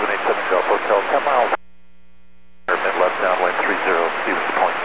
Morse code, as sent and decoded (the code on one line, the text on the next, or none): none